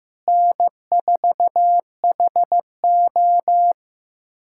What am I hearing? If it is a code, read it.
Morse code, heard as a keyed tone, about 15 wpm, 700 Hz. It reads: N4HO
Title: USA Callsign DE Callsign - 4 Characters - 15wpm